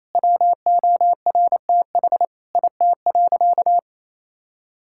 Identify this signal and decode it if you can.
Morse code, heard as a keyed tone, about 28 wpm, 700 Hz. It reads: WORTH IT.
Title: Sentences from Top 1000 Words - Review Part 5 - 28wpm